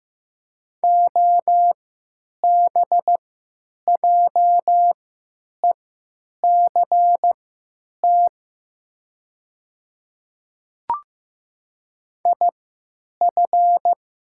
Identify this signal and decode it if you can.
Morse code, heard as a keyed tone, about 15 wpm, spaced as for 9 wpm, 700 Hz. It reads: OBJECT IF
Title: 3x Character Spacing - Top 500 Words - 15wpm